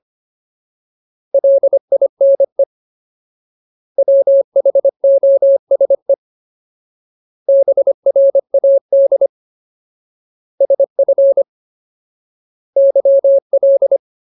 Morse code, heard as a keyed tone, about 25 wpm, 550 Hz. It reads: LINE WHOSE BRAD SF YL
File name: Ultimate Word List / 25WPM Ultimate